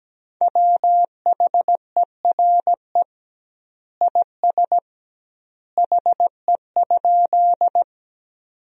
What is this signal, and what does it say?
Morse code, heard as a keyed tone, about 17 wpm, 700 Hz. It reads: WHERE IS HE?
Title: 2x Word-Spacing - Sentences from Top 200 Words - 17wpm